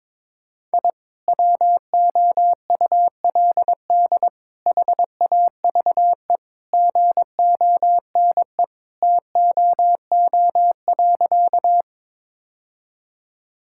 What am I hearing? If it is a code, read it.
Morse code, heard as a keyed tone, about 22 wpm, 700 Hz. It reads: I WOULD HAVE GONE TOO.
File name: practice / 22wpm / Sentences with 2-Word Phrases - 22wpm